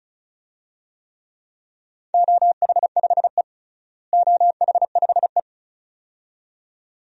Morse code, heard as a keyed tone, about 36 wpm, 700 Hz. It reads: OH5E OH5E
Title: DXCC-100 Call Signs - 4 Characters - Repeated 2x - 35wpm